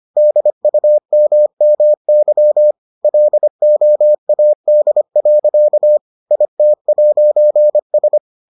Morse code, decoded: DUMMY LOAD. IT'S

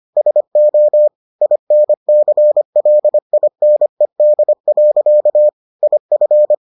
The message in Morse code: SO INCLINED. IF

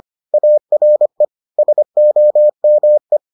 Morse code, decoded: ARE SOME

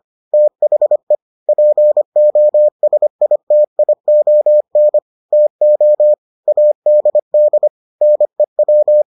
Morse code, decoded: THE POSITION TO ADD NEW

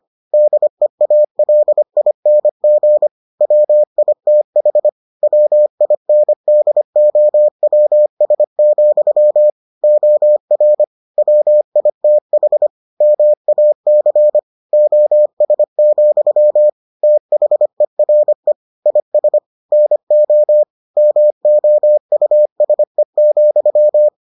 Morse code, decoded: DEALING WITH WINDOWS, OR WITH MAC OS, THERE IS NO MOUSE,

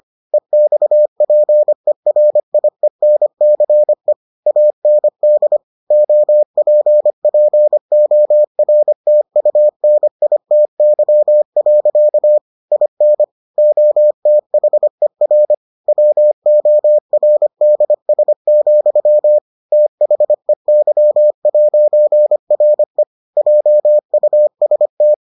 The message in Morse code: EXPERIENCE AND OPPORTUNITY. IN OTHER WORDS, THEY'RE JUST

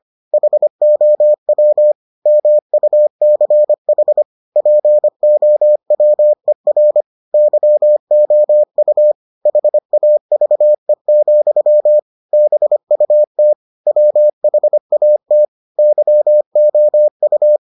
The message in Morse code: HOW MUCH POWER YOU HAVE, BUT WHAT YOU

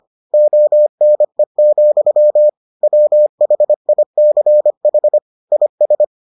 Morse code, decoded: ONE, WHICH IS

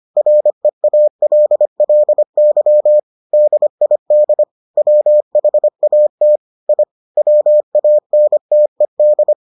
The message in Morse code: REALLY DID WHAT I WANTED

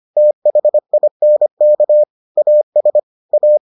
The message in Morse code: THINK AS A

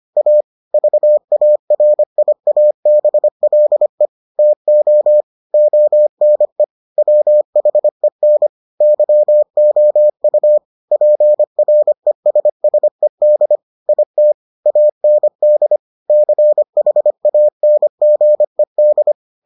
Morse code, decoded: A VARIABLE TO ONE WHEN YOU PRESSED IT AND CHANGED

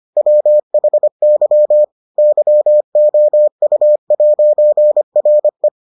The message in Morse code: WHY YOU'RE

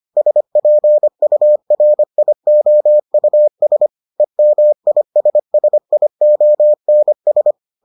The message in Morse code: SPURIOUS EMISSIONS

